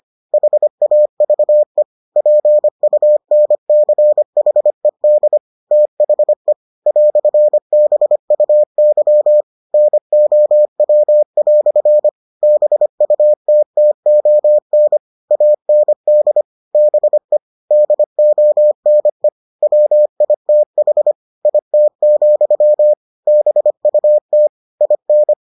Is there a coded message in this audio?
Yes: Morse code from 0.3 to 25.3 s, heard as a keyed tone, about 25 wpm, 600 Hz. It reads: HAVE PUNCHED THE "BUY NOW" BUTTON AND BE DONE WITH IT, BUT IN